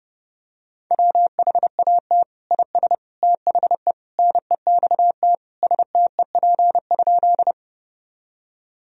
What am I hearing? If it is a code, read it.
Morse code, heard as a keyed tone, about 30 wpm, 700 Hz. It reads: WHAT IS THE NEXT STEP?
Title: Sentences from Top 1000 Words - Review Part 2 - 30wpm